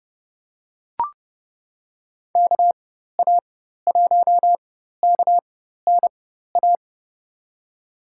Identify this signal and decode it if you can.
Morse code, heard as a keyed tone, about 31 wpm, spaced as for 14 wpm, 700 Hz. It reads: KA1KNA
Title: USA Call Signs - 6 Characters - 4x Character Spacing - 30wpm